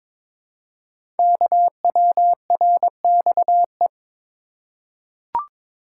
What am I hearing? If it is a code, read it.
Morse code, heard as a keyed tone, about 22 wpm, 700 Hz. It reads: KWRXE
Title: Code Groups - 5 Letters - 22wpm